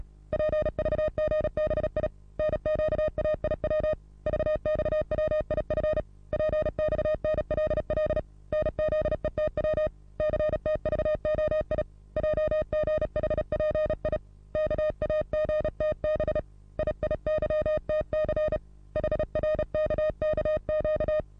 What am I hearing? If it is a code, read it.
Morse code, heard as a keyed tone, about 34 wpm, 620 Hz. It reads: PVGBI NQAIW VXWIF PXNLL NZETW CTVOI JGHPI KAGTB IIYTC HRKKQ
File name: repos / Morsekurs / DJ4UF Morsen27